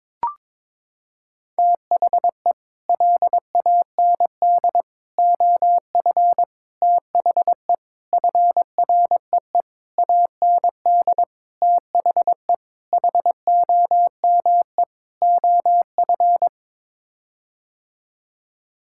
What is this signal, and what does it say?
Morse code, heard as a keyed tone, about 22 wpm, 700 Hz. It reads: THE LAND OF THE FREE AND THE HOME OF
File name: Sets of 9 Words - 22wpm